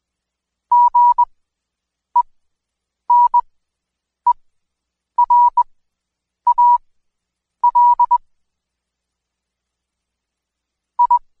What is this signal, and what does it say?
Morse code, heard as a keyed tone, about 20 wpm, spaced as for 8 wpm, 970 Hz. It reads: GENERAL I
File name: Code Quick / AudioDownld2 / Seg114-Day19-DayDrill 19